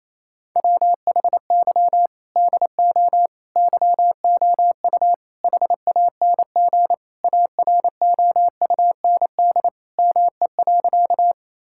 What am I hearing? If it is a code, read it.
Morse code, heard as a keyed tone, about 27 wpm, 700 Hz. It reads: WHY DO YOU HANG AROUND ME.
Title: Sentences from Top 1600 Words 28wpm